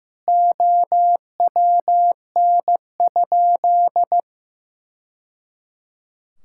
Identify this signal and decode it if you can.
Morse code, heard as a keyed tone, about 15 wpm, 700 Hz. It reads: OWN?
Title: Sentences from Top 1000 Words - Review Part 5 - 15wpm